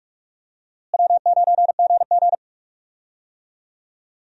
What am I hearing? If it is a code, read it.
Morse code, heard as a keyed tone, about 46 wpm, 700 Hz. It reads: W9GG
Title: USA Call Signs - 4 Characters - Repeated 3x - 45wpm